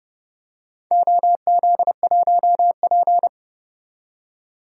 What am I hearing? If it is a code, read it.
Morse code, heard as a keyed tone, about 30 wpm, 700 Hz. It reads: OZ1P
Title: DXCC-100 Call Signs - 4 Characters - 30wpm